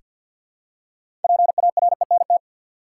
Morse code, heard as a keyed tone, about 46 wpm, 700 Hz. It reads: PARENT
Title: Six Letter Words - 50wpm